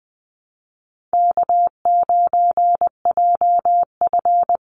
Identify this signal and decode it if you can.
Morse code, heard as a keyed tone, about 20 wpm, 700 Hz. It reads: K9JF
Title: Easy Call Signs Repeated 2x - 20wpm